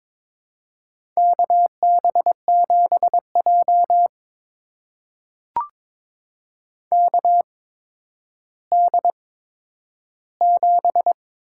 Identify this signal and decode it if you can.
Morse code, heard as a keyed tone, about 23 wpm, spaced as for 11 wpm, 700 Hz. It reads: KB7J KD7
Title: USA Call Signs - 8x Character Spacing - 22wpm